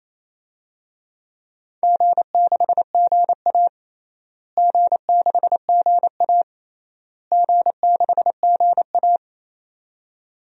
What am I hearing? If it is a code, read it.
Morse code, heard as a keyed tone, about 27 wpm, 700 Hz. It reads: G6GA G6GA G6GA
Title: DXCC-100 Call Signs - 4 Characters - Repeated 3x - 28wpm